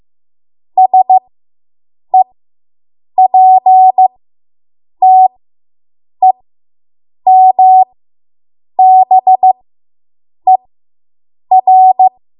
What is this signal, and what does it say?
Morse code, heard as a keyed tone, about 15 wpm, spaced as for 7 wpm, 750 Hz. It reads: SEPTEMBER